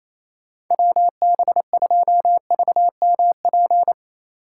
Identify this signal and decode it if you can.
Morse code, heard as a keyed tone, about 36 wpm, 700 Hz. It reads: WB2VMP